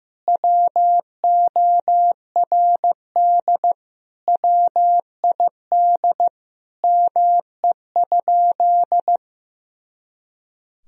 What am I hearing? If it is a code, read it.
Morse code, heard as a keyed tone, about 15 wpm, 700 Hz. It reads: WORD WID ME?
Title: Sentences from Top 300 Words and Abbreviations 15wpm